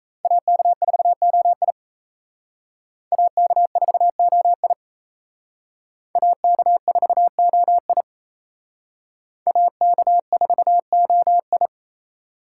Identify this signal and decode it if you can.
Morse code, heard as a keyed tone, about 36 wpm, 700 Hz. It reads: AK4OI AK4OI AK4OI AK4OI